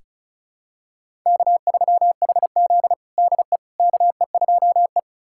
Morse code, heard as a keyed tone, about 36 wpm, 700 Hz. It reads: K3HZ DE KE2E
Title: USA Callsign DE Callsign - 4 Characters - 35wpm